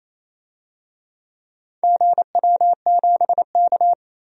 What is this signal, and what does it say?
Morse code, heard as a keyed tone, about 28 wpm, 700 Hz. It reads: GW7K
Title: DXCC-100 Call Signs - 4 Characters - 28wpm